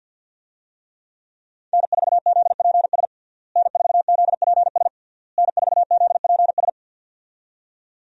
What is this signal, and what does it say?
Morse code, heard as a keyed tone, about 49 wpm, 700 Hz. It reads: N4ZPS N4ZPS N4ZPS
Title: USA Call Signs - 5 Characters - Repeated 3x - 50wpm